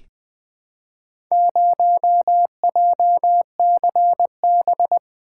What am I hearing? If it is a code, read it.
Morse code, heard as a keyed tone, about 20 wpm, 700 Hz. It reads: 0JCB
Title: Four Letter-Number Combinations - 20wpm